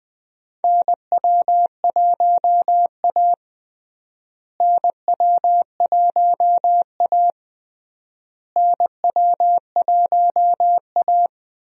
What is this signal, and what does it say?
Morse code, heard as a keyed tone, about 20 wpm, 700 Hz. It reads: NW1A NW1A NW1A
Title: USA Call Signs - 4 Characters - Repeated 3x - 20wpm